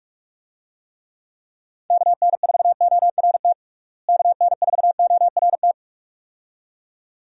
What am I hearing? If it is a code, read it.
Morse code, heard as a keyed tone, about 46 wpm, 700 Hz. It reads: KN4ORT KN4ORT
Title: USA Call Signs - 6 Characters - Repeated 2x - 45wpm